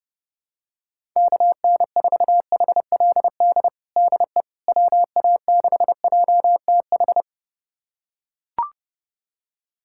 Morse code, heard as a keyed tone, about 32 wpm, 700 Hz. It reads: KN4HLD DE WA6JTH
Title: USA Callsign DE Callsign 30wpm